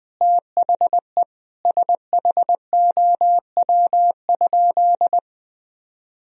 Morse code, heard as a keyed tone, about 21 wpm, 700 Hz. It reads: THE SHOW?
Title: Sentences from Top 1000 Words - Review Part 2 - 20wpm